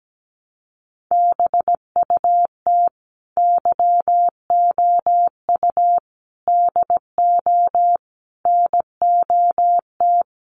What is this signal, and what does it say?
Morse code, heard as a keyed tone, about 17 wpm, 700 Hz. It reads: BUT YOU DO NOT
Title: Sets of 4 Words 17wpm